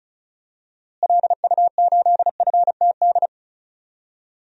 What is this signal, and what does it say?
Morse code, heard as a keyed tone, about 34 wpm, 700 Hz. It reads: LU8FTD
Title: DXCC-100 Call Signs - 6 Characters - Repeated 2x - 35wpm